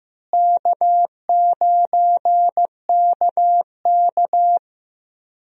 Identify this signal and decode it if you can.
Morse code, heard as a keyed tone, about 15 wpm, 700 Hz. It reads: K9KK
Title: USA Call Signs - 4 Characters - Repeated 3x - 15wpm